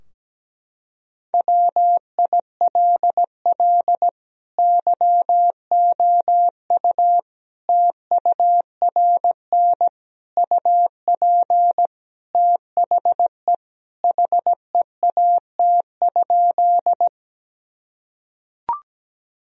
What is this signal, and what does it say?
Morse code, heard as a keyed tone, about 17 wpm, 700 Hz. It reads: WILL YOU TURN UP THE HEAT?
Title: Sentences from Top 1000 Words - Review Part 3 - 17wpm